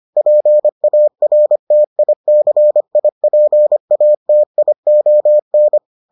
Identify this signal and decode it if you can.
Morse code, heard as a keyed tone, about 25 wpm, 600 Hz. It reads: PARTICIPATION